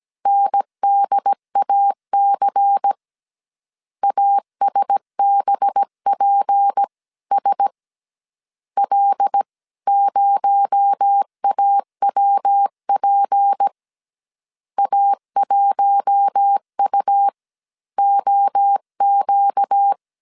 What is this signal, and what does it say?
Morse code, heard as a keyed tone, about 17 wpm, spaced as for 15 wpm, 800 Hz. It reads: NDAC ASBPS L0AWP A1UOQ
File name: CW / DET / 1 6 1500C S A 16wpm 2635